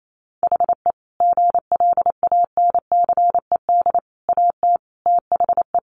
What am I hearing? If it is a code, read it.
Morse code, heard as a keyed tone, about 28 wpm, 700 Hz. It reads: HE GLANCED AT THE
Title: A Sound Of Thunder 28wpm